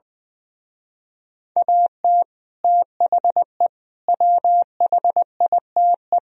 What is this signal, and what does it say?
Morse code, heard as a keyed tone, about 20 wpm, 700 Hz. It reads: AT THE WHITE
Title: Speed-Racing - Sets of 4 Words 20wpm